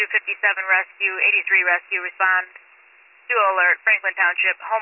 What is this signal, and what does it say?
Emergency Alert System (SAME) tones, heard as harsh buzzing bursts, with speech. It is heard from 0.6 to 4.0 s.